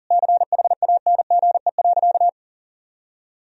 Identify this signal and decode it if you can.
Morse code, heard as a keyed tone, about 40 wpm, 700 Hz. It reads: CHANGE.
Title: Sentences from Top 1000 Words - Review Part 3 - 40wpm